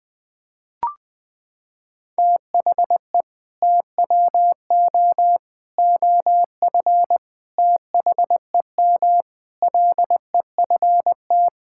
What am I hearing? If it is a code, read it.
Morse code, heard as a keyed tone, about 20 wpm, 700 Hz. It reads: THE TWO OF THEM LEFT
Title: Sentences with 4-Word Phrases - 20wpm